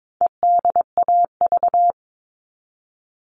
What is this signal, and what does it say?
Morse code, heard as a keyed tone, about 22 wpm, 700 Hz. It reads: EDAV
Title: All Cut Numbers - 4-Digit 22wpm